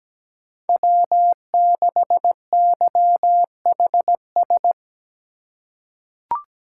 Morse code, heard as a keyed tone, about 17 wpm, 700 Hz. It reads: W6YHS
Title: USA Call Signs - 5 Characters - 2x Character Spacing - 17wpm